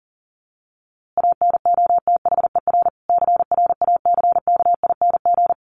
Morse code, heard as a keyed tone, about 37 wpm, 700 Hz. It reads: ANOTHER CRACKING
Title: A Sound Of Thunder 40wpm